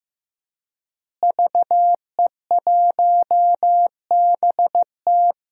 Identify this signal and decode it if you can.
Morse code, heard as a keyed tone, about 15 wpm, 700 Hz. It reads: VE1BT
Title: DXCC-100 Call Signs - 5 Characters - Repeated 2x - 15wpm